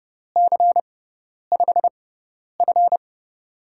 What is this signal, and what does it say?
Morse code, heard as a keyed tone, about 31 wpm, spaced as for 10 wpm, 690 Hz. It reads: C5F
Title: USA Call Signs - 4 Characters - 6x Character Spacing - 30wpm